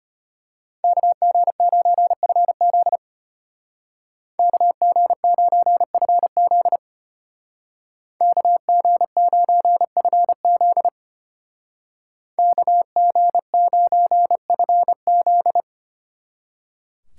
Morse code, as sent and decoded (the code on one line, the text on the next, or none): -.- --. ----. ..-. --.. / -.- --. ----. ..-. --.. / -.- --. ----. ..-. --.. / -.- --. ----. ..-. --..
KG9FZ KG9FZ KG9FZ KG9FZ